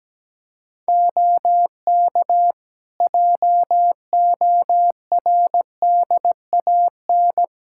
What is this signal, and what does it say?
Morse code, heard as a keyed tone, about 17 wpm, 700 Hz. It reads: OK JORDAN
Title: Practice QSO - Exchange 2 - Info - 17wpm